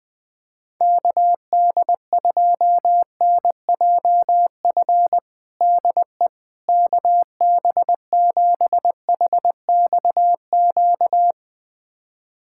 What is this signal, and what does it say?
Morse code, heard as a keyed tone, about 20 wpm, 700 Hz. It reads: KD2NJF DE KB7HXQ
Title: USA Callsign DE Callsign - 6 Characters - 20wpm